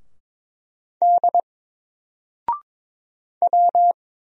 Morse code, heard as a keyed tone, about 22 wpm, 700 Hz. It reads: D W
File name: Single-Letter-Number-22wpm